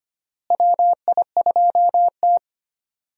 Morse code, heard as a keyed tone, about 25 wpm, 700 Hz. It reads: WI2T
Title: USA Call Signs - 4 Characters - Repeated 3x - 25wpm